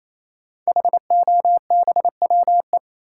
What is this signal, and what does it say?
Morse code, heard as a keyed tone, about 28 wpm, 700 Hz. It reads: HOBWE
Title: Code Groups - 5 Letters - 28wpm